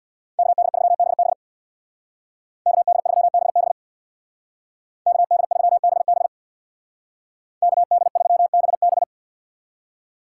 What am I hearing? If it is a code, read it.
Morse code, heard as a keyed tone, about 62 wpm, 700 Hz. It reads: KD3BB KD3BB KD3BB KD3BB
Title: Speed Racing - USA 2x2 Call Signs 50wpm